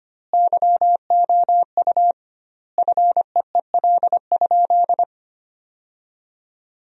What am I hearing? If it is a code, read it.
Morse code, heard as a keyed tone, about 25 wpm, 700 Hz. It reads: YOU FEEL?
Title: 2x Word-Spacing - Sentences from Top 400 Words - 25wpm